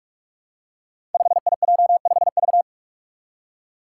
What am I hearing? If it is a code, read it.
Morse code, heard as a keyed tone, about 44 wpm, 700 Hz. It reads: 5IJ5V
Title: Five Challenging Letter-Number Combinations 45wpm